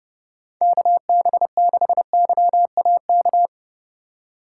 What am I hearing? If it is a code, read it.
Morse code, heard as a keyed tone, about 31 wpm, 700 Hz. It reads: KB6YAK